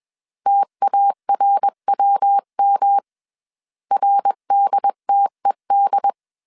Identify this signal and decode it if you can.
Morse code, heard as a keyed tone, about 21 wpm, 800 Hz. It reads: TARWM RDTED
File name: CW / DET / 1 3 1500C S B 20wpm 1754